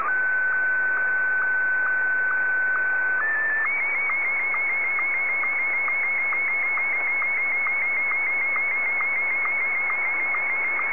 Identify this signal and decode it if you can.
SSTV (slow-scan television), Martin 1.
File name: sstv / M1-ZL2TJM